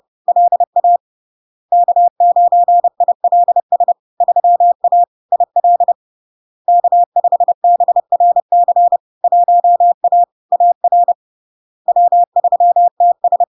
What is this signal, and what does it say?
Morse code, heard as a keyed tone, about 30 wpm, 700 Hz. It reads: LA K9ILS 3A IL K5BRC 1A AR W3TS
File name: LARK-FD-2022-Log-in-CW-30wpm-700-Hz